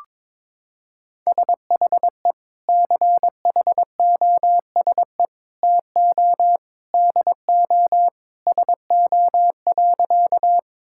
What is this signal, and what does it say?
Morse code, heard as a keyed tone, about 22 wpm, 700 Hz. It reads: SHE CHOSE TO DO SO.
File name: practice / 22wpm / Sentences with 3-Word Phrases - 22wpm